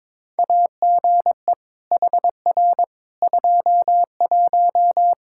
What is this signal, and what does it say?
Morse code, heard as a keyed tone, about 22 wpm, 700 Hz. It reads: AGE HR 21
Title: K1USN SST QSO Practice 22wpm